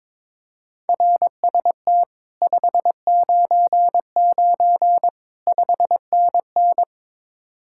Morse code, heard as a keyed tone, about 22 wpm, 700 Hz. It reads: RST 599 5NN
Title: Speed-Racing Common QSO Phrases - 22wpm